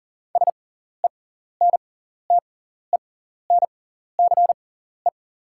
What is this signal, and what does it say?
Morse code, heard as a keyed tone, about 40 wpm, spaced as for 14 wpm, 690 Hz. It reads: SENTENCE
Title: 6x Character Spacing - Top 200 Words - 40wpm